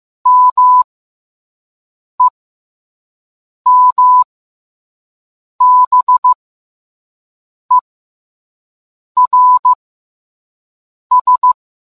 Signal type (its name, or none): Morse code